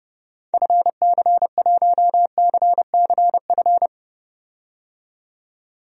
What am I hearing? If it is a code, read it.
Morse code, heard as a keyed tone, about 30 wpm, 700 Hz. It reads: FC1CCF
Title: DXCC-100 Call Signs - 6 Characters - 30wpm